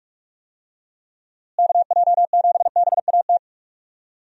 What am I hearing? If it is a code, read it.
Morse code, heard as a keyed tone, about 44 wpm, 700 Hz. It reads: KJ7BAT
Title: USA Call Signs - 6 Characters - Repeated 3x - 45wpm